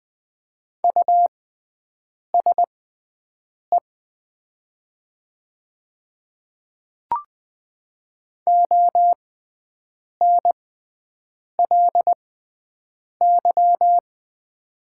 Morse code, heard as a keyed tone, about 20 wpm, spaced as for 7 wpm, 700 Hz. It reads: USE ONLY
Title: 6x Character Spacing - Top 200 Words - 20wpm